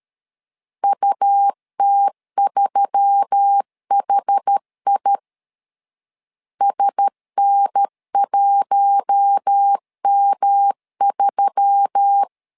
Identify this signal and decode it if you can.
Morse code, heard as a keyed tone, about 13 wpm, 800 Hz. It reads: UT3HI SN1M3
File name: LEZIONI / 4 Q1S3 12wpm 2023